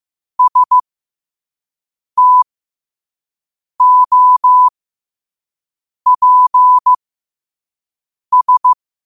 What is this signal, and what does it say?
Morse code, heard as a keyed tone, about 15 wpm, spaced as for 5 wpm, 1000 Hz. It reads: STOPS
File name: qotdfiles / 5wpm